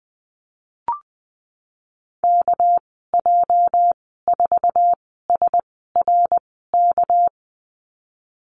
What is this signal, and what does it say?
Morse code, heard as a keyed tone, about 20 wpm, spaced as for 14 wpm, 700 Hz. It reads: KJ4SRK
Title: 219 Lesson 35g - Number 0 - Call Signs 20wpm